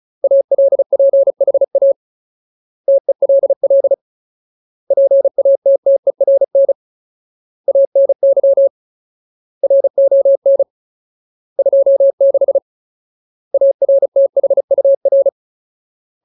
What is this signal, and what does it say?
Morse code, heard as a keyed tone, about 34 wpm, 550 Hz. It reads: ALPHA TELL PATTERN ANY RON 26 ARTHUR
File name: Ultimate Word List / 35WPM Ultimate